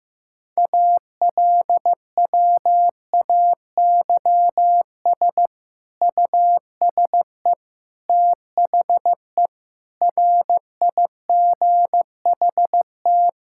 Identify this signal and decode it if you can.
Morse code, heard as a keyed tone, about 16 wpm, 700 Hz. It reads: ALWAYS USE THE RIGHT
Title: Sentences from Top 1000 Words - Review Part 6 - 15wpm